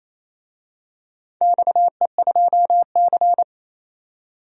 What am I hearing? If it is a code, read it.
Morse code, heard as a keyed tone, about 28 wpm, 700 Hz. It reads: XE2C
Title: DXCC-100 Call Signs - 4 Characters - 28wpm